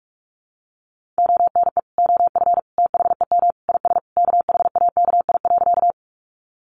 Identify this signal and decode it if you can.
Morse code, heard as a keyed tone, about 44 wpm, 700 Hz. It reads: ONE OF THEM IS KHAKI.
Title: Additional Easy Phrases 45wpm